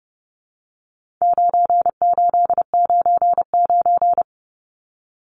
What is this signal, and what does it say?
Morse code, heard as a keyed tone, about 30 wpm, 700 Hz. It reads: 9899
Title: Numbers - 4-Digit 30wpm